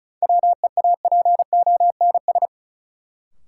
Morse code, heard as a keyed tone, about 36 wpm, 700 Hz. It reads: WEAPONS
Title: Sets of 5 Words Encore - New Word Drill - 35wpm